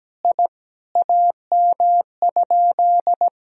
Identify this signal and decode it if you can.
Morse code, heard as a keyed tone, about 17 wpm, 700 Hz. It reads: I AM?
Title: Sentences from Top 1000 Words - Review Part 2 - 17wpm